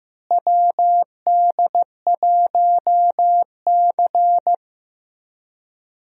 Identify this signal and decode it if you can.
Morse code, heard as a keyed tone, about 15 wpm, 700 Hz. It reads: WD1C
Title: USA Call Signs - 4 Characters - Repeated 2x - 15wpm